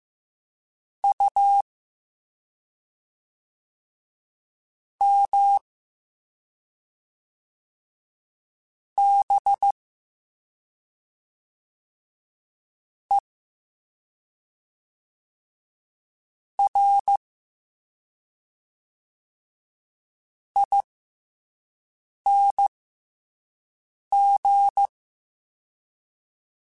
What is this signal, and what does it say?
Morse code, heard as a keyed tone, about 15 wpm, spaced as for 5 wpm, 800 Hz. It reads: U M B E R ING